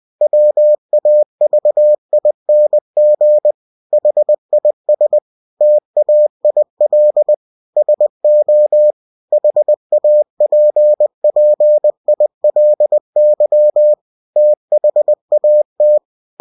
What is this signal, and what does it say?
Morse code, heard as a keyed tone, about 20 wpm, 600 Hz. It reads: WAVING HIS TAIL SO HAPPILY THAT
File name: cw-books / winnie-the-pooh / winnie-20wpm-page-0032